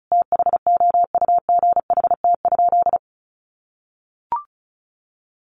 Morse code, heard as a keyed tone, about 34 wpm, 700 Hz. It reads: THOUGHT?
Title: Sentences from Top 300 Words 35wpm